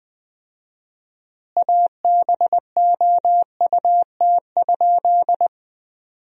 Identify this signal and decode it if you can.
Morse code, heard as a keyed tone, about 21 wpm, 700 Hz. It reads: ABOUT?
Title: 4x Word-Spacing - Sentences from Top 100 Words - 20wpm